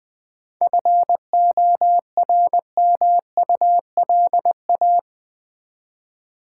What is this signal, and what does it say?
Morse code, heard as a keyed tone, about 20 wpm, 700 Hz. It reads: FORMULA
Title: 7 Letter Words - First 3 Letters Given - 20wpm